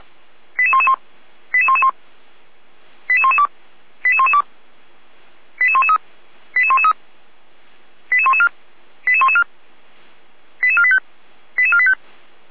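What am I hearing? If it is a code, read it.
Five-tone selective call (ZVEI), heard as quick runs of five tones, tones of about 70 ms. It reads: 88111 88111 88112 88112 88113 88113 88114 88114 88485 88485
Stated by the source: ZVEI five-tone selective call received on an SDR.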